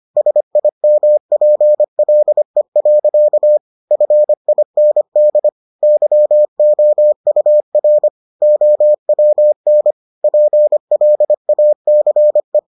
Morse code, heard as a keyed tone, about 25 wpm, 600 Hz. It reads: SIMPLE. FIND YOUR OWN PLACE